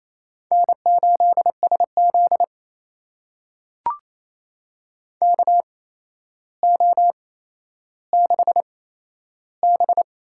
Morse code, heard as a keyed tone, about 28 wpm, spaced as for 12 wpm, 700 Hz. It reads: N8SZ KO6B
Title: USA Call Signs - 4 Characters - 8x Character Spacing - 28wpm